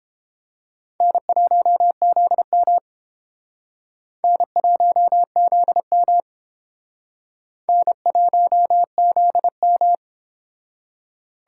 Morse code, heard as a keyed tone, about 29 wpm, 700 Hz. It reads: N1ZM N1ZM N1ZM